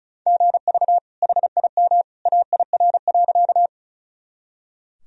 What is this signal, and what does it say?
Morse code, heard as a keyed tone, about 34 wpm, 700 Hz. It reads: GV HIM AIR.
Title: Sentences from Top 200 Words and Abbreviations 35wpm